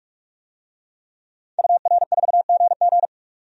Morse code, heard as a keyed tone, about 46 wpm, 700 Hz. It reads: UR4GG
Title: DXCC-100 Call Signs - 5 Characters - 45wpm